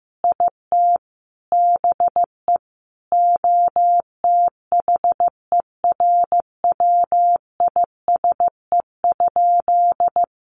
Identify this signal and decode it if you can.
Morse code, heard as a keyed tone, about 15 wpm, 700 Hz. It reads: IT BE OTHERWISE?
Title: Sentences from Top 1200 Words 15wpm